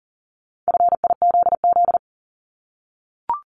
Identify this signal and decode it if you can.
Morse code, heard as a keyed tone, about 42 wpm, 690 Hz. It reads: FIZZ
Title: Multi-letter Phonemes 40wpm